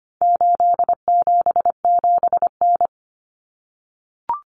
Morse code, heard as a keyed tone, about 25 wpm, 700 Hz. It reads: 877N
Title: Common Cut Numbers - 4-Digit 25wpm